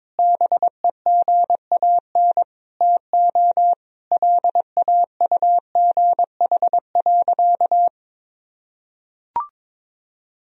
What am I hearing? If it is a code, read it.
Morse code, heard as a keyed tone, about 21 wpm, 700 Hz. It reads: BEGAN TO LAUGH.
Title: Sentences from Top 1000 Words - Review Part 2 - 22wpm